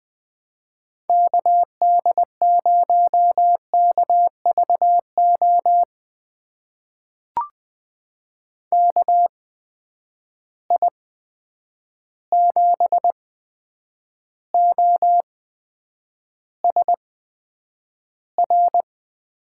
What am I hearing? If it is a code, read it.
Morse code, heard as a keyed tone, about 20 wpm, spaced as for 9 wpm, 700 Hz. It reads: KD0KVO KI7OSR
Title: USA Call Signs - 6 Characters - 8x Character Spacing - 20wpm